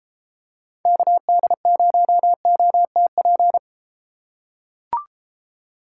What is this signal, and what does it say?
Morse code, heard as a keyed tone, about 33 wpm, 700 Hz. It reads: KD0OTP